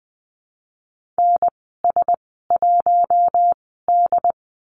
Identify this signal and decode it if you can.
Morse code, heard as a keyed tone, about 20 wpm, spaced as for 14 wpm, 700 Hz. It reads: NS1D
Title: 082 Lesson 14g - Number 2 - Call Signs 20wpm